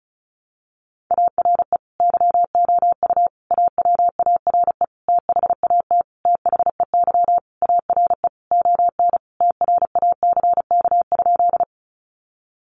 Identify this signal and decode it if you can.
Morse code, heard as a keyed tone, about 34 wpm, 700 Hz. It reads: ARE YOU AWARE THAT THEY ARE ON TRACK?
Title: Sentences from Top 1200 Words 35wpm